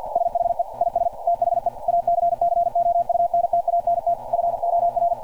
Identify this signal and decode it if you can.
Morse code, heard as a keyed tone, about 48 wpm, 690 Hz. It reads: SHE IS THE ACCOUNTANT FOR
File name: archived / CRSnet